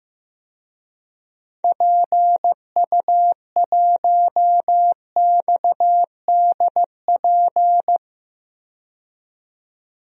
Morse code, heard as a keyed tone, about 15 wpm, 700 Hz. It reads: PU1XDP
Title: DXCC-100 Call Signs - 6 Characters - 15wpm